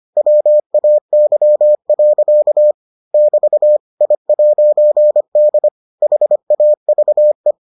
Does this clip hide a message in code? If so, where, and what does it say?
Yes: Morse code from 0.2 to 7.5 s, heard as a keyed tone, about 25 wpm, 600 Hz. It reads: WAY. <BT> I'D HAVE